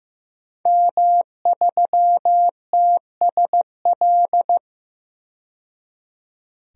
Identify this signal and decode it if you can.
Morse code, heard as a keyed tone, about 15 wpm, 700 Hz. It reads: M3TSL
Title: DXCC-100 Call Signs - 5 Characters - 15wpm